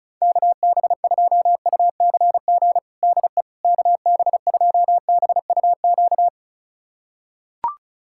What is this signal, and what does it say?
Morse code, heard as a keyed tone, about 34 wpm, 700 Hz. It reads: KB2UCG DE KB2BUQ
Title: USA Callsign DE Callsign - 6 Characters - 35wpm